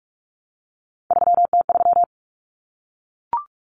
Morse code, heard as a keyed tone, about 44 wpm, 700 Hz. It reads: 3T3
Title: Common Cut Numbers - 3-Digit 45wpm